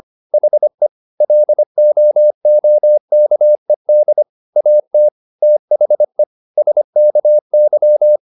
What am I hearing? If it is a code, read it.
Morse code, heard as a keyed tone, about 25 wpm, 600 Hz. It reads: HE LOOKED AT THE SKY
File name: the old man and the sea part 1 25wpm